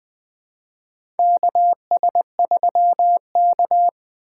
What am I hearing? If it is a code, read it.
Morse code, heard as a keyed tone, about 20 wpm, 700 Hz. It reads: KS3K
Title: USA Call Signs - 4 Characters - 3x Character Spacing - 20wpm